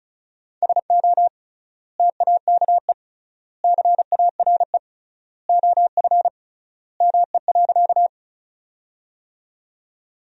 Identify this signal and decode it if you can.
Morse code, heard as a keyed tone, about 36 wpm, 700 Hz. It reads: SO TAKE CARE OF ME.
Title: 3x Word-Spacing - Sentences from Top 300 Words - 35wpm